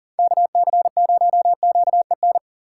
Morse code, heard as a keyed tone, about 50 wpm, 690 Hz. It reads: KC0QEN
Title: Speed Racing - USA Call Signs 40wpm